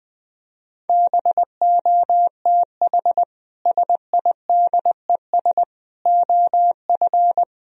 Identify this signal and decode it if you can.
Morse code, heard as a keyed tone, about 20 wpm, 700 Hz. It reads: BOTH SIDES OF
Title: Speed-Racing - Sets of 4 Words 20wpm